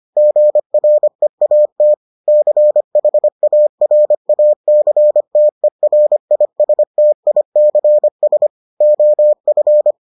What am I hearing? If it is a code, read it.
Morse code, heard as a keyed tone, about 25 wpm, 600 Hz. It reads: GREAT CHARACTERISTICS OF